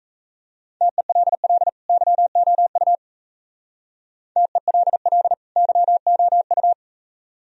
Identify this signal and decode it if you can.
Morse code, heard as a keyed tone, about 35 wpm, 700 Hz. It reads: TELL YOU TELL YOU
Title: Speed-Racing - Sets of 2 Words 28wpm